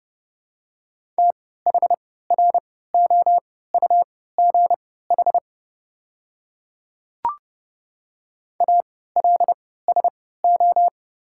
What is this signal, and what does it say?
Morse code, heard as a keyed tone, about 30 wpm, spaced as for 17 wpm, 700 Hz. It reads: THROUGH ALSO
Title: 3x Character Spacing - Top 500 Words - 30wpm